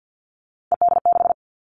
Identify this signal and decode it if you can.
Morse code, heard as a keyed tone, about 50 wpm, 700 Hz. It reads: ED6
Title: Callsign Prefixes for Top 50 DXCC - 50wpm